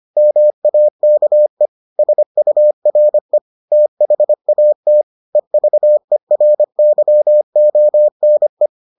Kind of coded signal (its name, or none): Morse code